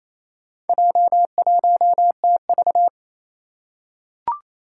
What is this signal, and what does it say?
Morse code, heard as a keyed tone, about 27 wpm, 700 Hz. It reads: J1TV
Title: Four Challenging Letter-Number Combinations 28wpm